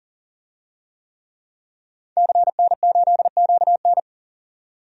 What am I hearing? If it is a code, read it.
Morse code, heard as a keyed tone, about 39 wpm, 700 Hz. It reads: CN8QN
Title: DXCC-100 Call Signs - 5 Characters - 40wpm